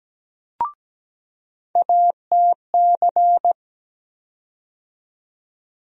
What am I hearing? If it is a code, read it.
Morse code, heard as a keyed tone, about 17 wpm, 700 Hz. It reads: ATC
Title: USA Call Sign Postfixes - 3 Letters 17wpm